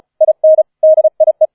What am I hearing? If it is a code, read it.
Morse code, heard as a keyed tone, about 34 wpm, 630 Hz. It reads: IN DIE